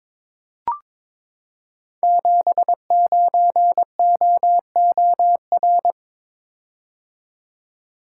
Morse code, heard as a keyed tone, about 22 wpm, 700 Hz. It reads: 79OOR
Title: Code Groups - 5 Letter:Numbers - 22wpm